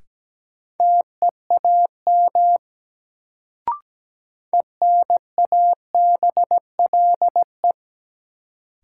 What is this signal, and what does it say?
Morse code, heard as a keyed tone, about 17 wpm, 700 Hz. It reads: TEAM ENABLE
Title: Sentences from Top 1500 Words - New Word Drill - 17wpm